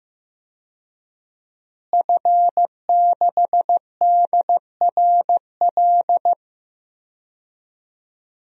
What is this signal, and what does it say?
Morse code, heard as a keyed tone, about 15 wpm, 700 Hz. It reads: F6DRL
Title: DXCC-100 Call Signs - 5 Characters - Repeated 3x - 15wpm